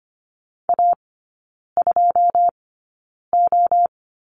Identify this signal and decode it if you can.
Morse code, heard as a keyed tone, about 25 wpm, spaced as for 9 wpm, 700 Hz. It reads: A2O
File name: Lesson 14e - Number 2 - Three Characters 25wpm